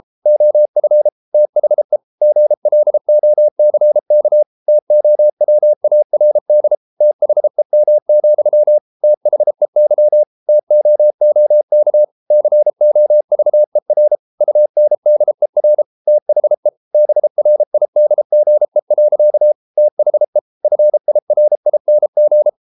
Morse code, heard as a keyed tone, about 33 wpm, 600 Hz. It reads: OF THE GLOCK TOWARD THEM, THEY TOOK COVER UNDER THE BRIDGE. THE FIRING